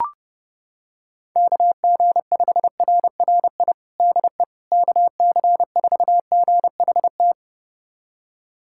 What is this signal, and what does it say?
Morse code, heard as a keyed tone, about 32 wpm, 700 Hz. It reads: KG5RRI DE KC4GHT
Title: USA Callsign DE Callsign - 6 Characters - 30wpm